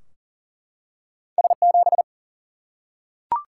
Morse code, heard as a keyed tone, about 40 wpm, 700 Hz. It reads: S7
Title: Single-Letter Single-Number 40wpm